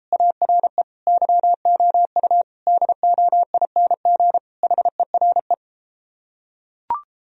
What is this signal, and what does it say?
Morse code, heard as a keyed tone, about 33 wpm, 700 Hz. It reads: ARE YOU DOING HERE